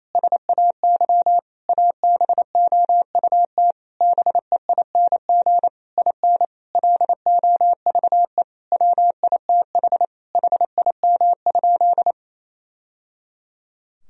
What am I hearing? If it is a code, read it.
Morse code, heard as a keyed tone, about 28 wpm, 700 Hz. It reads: SAY ABOUT BEING IN LOVE WITH HIM?
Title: Sentences from Top 1000 Words - Review Part 3 - 28wpm